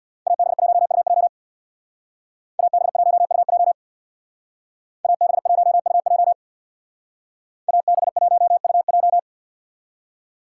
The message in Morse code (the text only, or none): AB1UJ AB1UJ AB1UJ AB1UJ